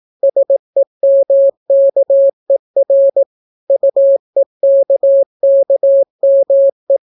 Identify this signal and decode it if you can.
Morse code, heard as a keyed tone, about 18 wpm, 550 Hz. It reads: SEMKER UEKKME